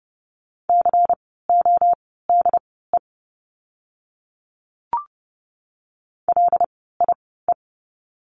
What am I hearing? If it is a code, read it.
Morse code, heard as a keyed tone, about 30 wpm, spaced as for 17 wpm, 700 Hz. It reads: CODE LIE
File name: Lesson 16f - Letter c - Words 30wpm